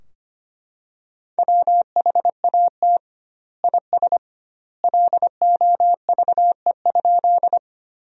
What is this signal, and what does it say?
Morse code, heard as a keyed tone, about 25 wpm, 700 Hz. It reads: WHAT IS LOVE?
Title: 2x Word-Spacing - Sentences from Top 400 Words - 25wpm